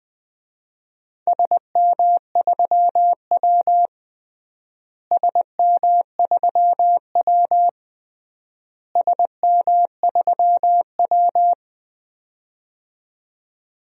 Morse code, heard as a keyed tone, about 20 wpm, 700 Hz. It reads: SM3W SM3W SM3W
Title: DXCC-100 Call Signs - 4 Characters - Repeated 3x - 20wpm